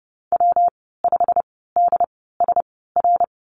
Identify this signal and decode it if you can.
Morse code, heard as a keyed tone, about 30 wpm, spaced as for 17 wpm, 700 Hz. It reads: W5DSR
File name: course / 30:10wpm / 116 Lesson 19g - Letter w - Call Signs 30wpm